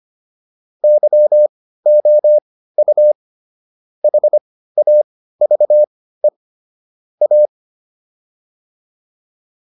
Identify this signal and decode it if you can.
Morse code, heard as a keyed tone, about 25 wpm, spaced as for 15 wpm, 600 Hz. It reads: YOU HAVE A